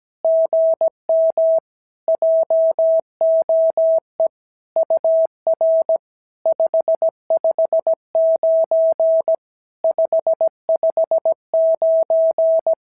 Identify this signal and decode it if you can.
Morse code, heard as a keyed tone, about 17 wpm, 650 Hz. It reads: GM JOE UR 559 559
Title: SOTA QSO Practice 17wpm